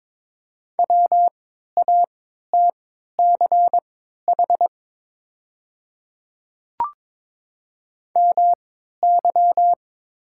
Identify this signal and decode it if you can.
Morse code, heard as a keyed tone, about 22 wpm, spaced as for 13 wpm, 690 Hz. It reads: WATCH MY
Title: 3x Character Spacing - Top 300 Words - 22wpm